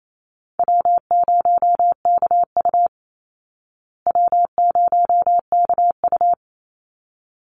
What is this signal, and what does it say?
Morse code, heard as a keyed tone, about 28 wpm, 700 Hz. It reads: W0KU W0KU
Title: Easy Call Signs Repeated 2x - 28wpm